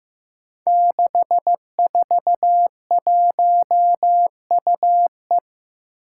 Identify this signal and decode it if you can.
Morse code, heard as a keyed tone, about 16 wpm, 700 Hz. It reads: 641UE
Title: Five Challenging Letter-Number Combinations 15wpm